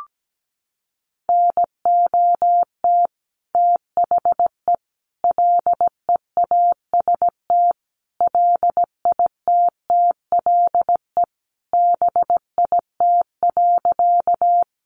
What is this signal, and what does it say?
Morse code, heard as a keyed tone, about 17 wpm, 700 Hz. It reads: NOT THE LEAST LITTLE BIT.
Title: Sentences from Top 900 Words 17wpm